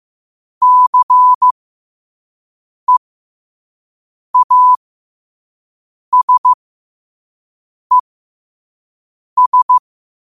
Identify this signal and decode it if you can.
Morse code, heard as a keyed tone, about 15 wpm, spaced as for 5 wpm, 1000 Hz. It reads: CEASES
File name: qotdfiles / 5wpm